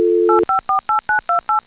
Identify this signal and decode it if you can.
DTMF, heard as telephone touch tones, tones of about 130 ms. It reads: 8548928